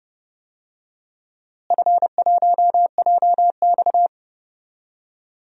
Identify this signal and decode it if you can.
Morse code, heard as a keyed tone, about 30 wpm, 700 Hz. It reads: F1JX
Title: DXCC-100 Call Signs - 4 Characters - Repeated 2x - 30wpm